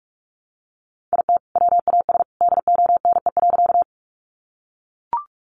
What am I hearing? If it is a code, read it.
Morse code, heard as a keyed tone, about 44 wpm, 700 Hz. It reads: IT WAS DONE.
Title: Sentences from Top 500 Words 45wpm